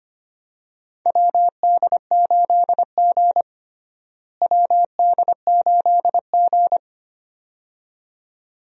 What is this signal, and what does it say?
Morse code, heard as a keyed tone, about 25 wpm, 700 Hz. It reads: WD8G WD8G
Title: USA Call Signs - 4 Characters - Repeated 2x - 25wpm